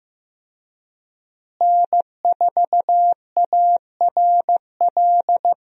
Morse code, heard as a keyed tone, about 15 wpm, 700 Hz. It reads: N4ARL
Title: USA Call Signs - 5 Characters - Repeated 2x - 15wpm